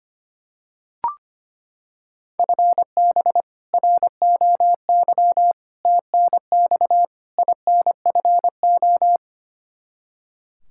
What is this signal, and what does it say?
Morse code, heard as a keyed tone, about 25 wpm, 700 Hz. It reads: FB ROY TNX INFO
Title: K1USN SST QSO Practice 25wpm